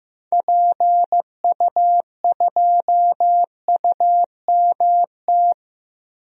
Morse code, heard as a keyed tone, about 15 wpm, 700 Hz. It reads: PU2UMT
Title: DXCC-100 Call Signs - 6 Characters - Repeated 3x - 15wpm